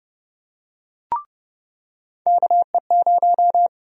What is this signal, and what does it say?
Morse code, heard as a keyed tone, about 30 wpm, 700 Hz. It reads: KE0
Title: USA Call Sign Prefixes - 30wpm